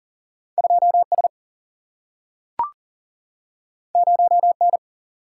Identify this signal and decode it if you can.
Morse code, heard as a keyed tone, about 40 wpm, 700 Hz. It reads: 2 S 0 N
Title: Single-Letter Single-Number 40wpm